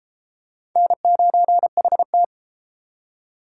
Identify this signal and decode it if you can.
Morse code, heard as a keyed tone, about 29 wpm, 700 Hz. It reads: N9HT